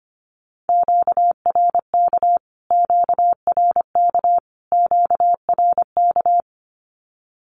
Morse code, heard as a keyed tone, about 25 wpm, 700 Hz. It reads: QRK QRK QRK
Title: Q-Codes Repeated 3x 25wpm